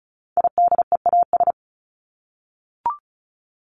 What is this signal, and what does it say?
Morse code, heard as a keyed tone, about 34 wpm, 700 Hz. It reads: IDEAS
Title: ES:S Suffix - Top 100 Words with Base Words 35wpm